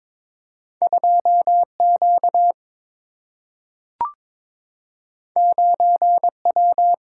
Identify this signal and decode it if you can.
Morse code, heard as a keyed tone, about 22 wpm, 700 Hz. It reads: 2Q 9W
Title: Single-Letter Single-Number 22wpm